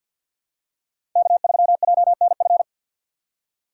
Morse code, heard as a keyed tone, about 51 wpm, 700 Hz. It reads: K3JNF
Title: USA Call Signs - 3x Character Spacing - 50wpm